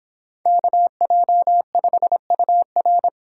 Morse code, heard as a keyed tone, about 29 wpm, 700 Hz. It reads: KJ5UR